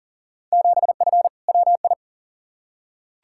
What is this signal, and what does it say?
Morse code, heard as a keyed tone, about 40 wpm, 700 Hz. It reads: 7F WI
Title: Field Day Exchanges Encore - 40wpm